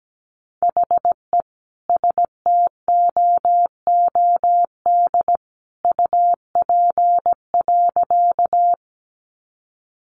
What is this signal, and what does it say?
Morse code, heard as a keyed tone, about 17 wpm, 700 Hz. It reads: HE STOOD UP.
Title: Sentences from Top 500 Words 17wpm